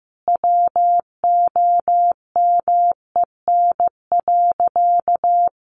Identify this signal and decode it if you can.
Morse code, heard as a keyed tone, about 15 wpm, 700 Hz. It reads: WOMEN.
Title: Sentences from Top 1400 Words 15wpm